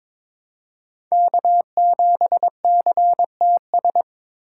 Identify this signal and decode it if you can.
Morse code, heard as a keyed tone, about 22 wpm, 700 Hz. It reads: K7CTS
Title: USA Call Signs - 5 Characters - 2x Character Spacing - 22wpm